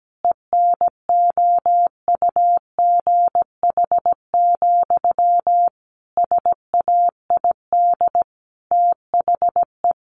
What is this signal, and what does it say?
Morse code, heard as a keyed tone, about 17 wpm, 700 Hz. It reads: ENOUGH, SAID THE
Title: Sentences from Top 600 Words 17wpm